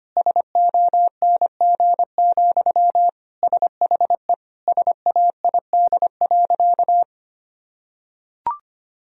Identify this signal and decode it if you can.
Morse code, heard as a keyed tone, about 25 wpm, 700 Hz. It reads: SONG, SHE SAID.
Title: Sentences from Top 1000 Words - Review Part 2 - 25wpm